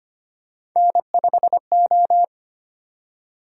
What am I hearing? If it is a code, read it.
Morse code, heard as a keyed tone, about 25 wpm, 700 Hz. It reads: N5O
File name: Three Letter-Number Combinations - 25wpm